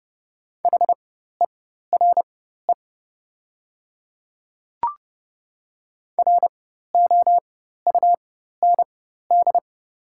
Morse code, heard as a keyed tone, about 30 wpm, spaced as for 14 wpm, 700 Hz. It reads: HERE ROUND
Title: 4x Character Spacing - Top 300 Words - 30wpm